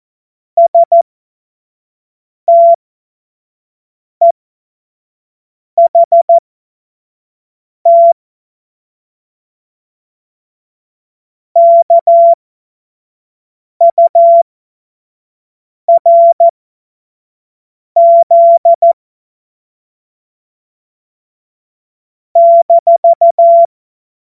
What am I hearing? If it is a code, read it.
Morse code, heard as a keyed tone, about 14 wpm, spaced as for 5 wpm, 680 Hz. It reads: STEHT KURZ -
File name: Witz-19-August-2025-25-F70